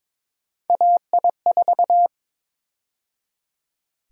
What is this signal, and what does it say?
Morse code, heard as a keyed tone, about 21 wpm, 700 Hz. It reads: AI4
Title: USA Call Sign Prefixes - 22wpm